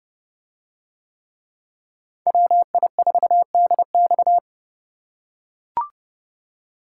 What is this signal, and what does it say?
Morse code, heard as a keyed tone, about 28 wpm, 700 Hz. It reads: WI4DX